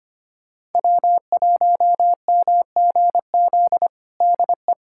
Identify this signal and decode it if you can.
Morse code, heard as a keyed tone, about 25 wpm, 700 Hz. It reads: W1MGZ DE
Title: USA Callsign DE Callsign - 5 Characters - 25wpm